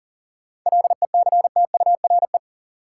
Morse code, heard as a keyed tone, about 40 wpm, 700 Hz. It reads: LECTURE
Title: Sentences from Top 1600 Words - New Word Drill Alt - 40wpm